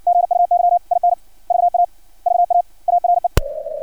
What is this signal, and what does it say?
Morse code, heard as a keyed tone, about 60 wpm, 700 Hz. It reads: GUY IN VA VA ARE
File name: archived / CRSnet